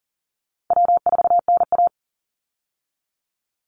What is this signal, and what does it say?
Morse code, heard as a keyed tone, about 40 wpm, 700 Hz. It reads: W4NA
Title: Call Signs Repeated 3x - 40wpm